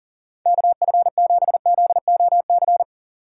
Morse code, heard as a keyed tone, about 50 wpm, 700 Hz. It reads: KF7ZOC
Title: Speed Racing - USA 2x3 Call Signs 40wpm